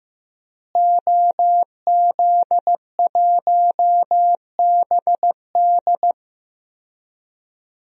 Morse code, heard as a keyed tone, about 15 wpm, 700 Hz. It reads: OZ1BD
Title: DXCC-100 Call Signs - 5 Characters - Repeated 2x - 15wpm